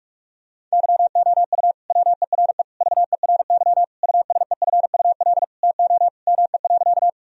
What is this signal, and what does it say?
Morse code, heard as a keyed tone, about 46 wpm, 700 Hz. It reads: YOU WERE VERY USEFUL TO ME.
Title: Sentences from Top 1000 Words - Review Part 4 - 45wpm